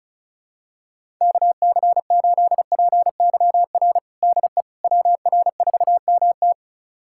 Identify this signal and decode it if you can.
Morse code, heard as a keyed tone, about 34 wpm, 700 Hz. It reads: KC8PYR DE WR4MT
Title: USA Callsign DE Callsign 35wpm